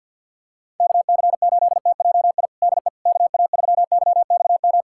Morse code, heard as a keyed tone, about 52 wpm, 700 Hz. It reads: KC8TJI DE KA3YXM
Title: USA Callsign DE Callsign - 6 Characters - 50wpm